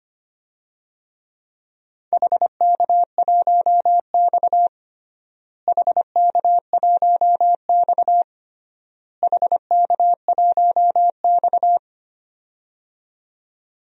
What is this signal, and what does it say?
Morse code, heard as a keyed tone, about 25 wpm, 700 Hz. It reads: HK1X HK1X HK1X
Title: DXCC-100 Call Signs - 4 Characters - Repeated 3x - 25wpm